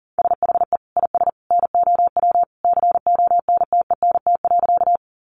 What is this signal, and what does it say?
Morse code, heard as a keyed tone, about 40 wpm, 700 Hz. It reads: SHE IS NOW CONTENT.
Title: Sentences from Top 900 Words 40wpm